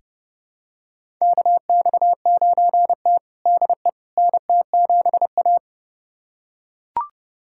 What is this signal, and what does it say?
Morse code, heard as a keyed tone, about 30 wpm, 700 Hz. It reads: KX9T DE NT7A
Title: USA Callsign DE Callsign - 4 Characters - 30wpm